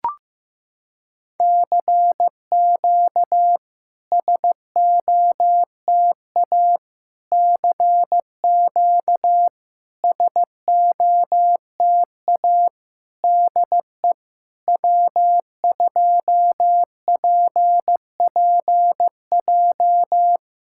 Morse code, heard as a keyed tone, about 15 wpm, 700 Hz. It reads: CQ SOTA CQ SOTA DE W2PPJ
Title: SOTA QSO Practice 15wpm